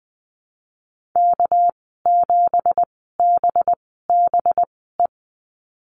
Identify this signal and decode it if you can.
Morse code, heard as a keyed tone, about 20 wpm, spaced as for 15 wpm, 700 Hz. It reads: K7BBE
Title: 213 Lesson 34g - Number 8 - Call Signs 20wpm